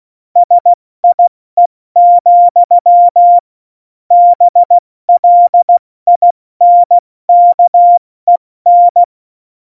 Morse code, heard as a keyed tone, about 16 wpm, 700 Hz. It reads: SIE, BLINKEN
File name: Morsekurs / Witze / Witz 05. Februar 2019, 70 F80